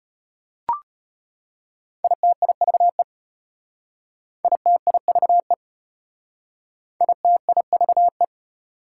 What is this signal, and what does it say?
Morse code, heard as a keyed tone, about 32 wpm, 700 Hz. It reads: ITIVE ITIVE ITIVE